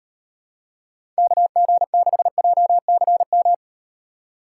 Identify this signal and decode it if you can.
Morse code, heard as a keyed tone, about 38 wpm, 700 Hz. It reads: KG6JCM